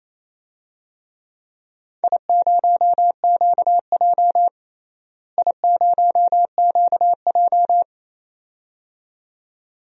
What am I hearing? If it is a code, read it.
Morse code, heard as a keyed tone, about 28 wpm, 700 Hz. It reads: I0QJ I0QJ
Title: DXCC-100 Call Signs - 4 Characters - Repeated 2x - 28wpm